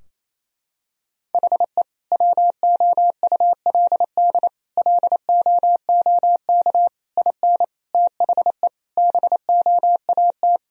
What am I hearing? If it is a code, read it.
Morse code, heard as a keyed tone, about 27 wpm, 700 Hz. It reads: HE WOULD LOOK IN THE BOAT
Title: Sentences from Top 1000 Words - Review Part 3 - 28wpm